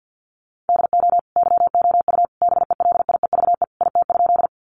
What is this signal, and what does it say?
Morse code, heard as a keyed tone, about 50 wpm, 700 Hz. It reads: DO YOU BELIEVE IT?
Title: Sentences from Top 1000 Words 50wpm